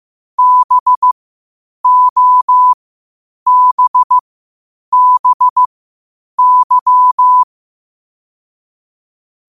Morse code, heard as a keyed tone, about 15 wpm, spaced as for 8 wpm, 1000 Hz. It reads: BOBBY
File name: qotdfiles / 8wpm